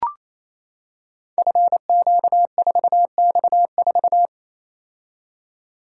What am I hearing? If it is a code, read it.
Morse code, heard as a keyed tone, about 29 wpm, 700 Hz. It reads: FQ4X4
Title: Code Groups - 5 Letter:Numbers - 28wpm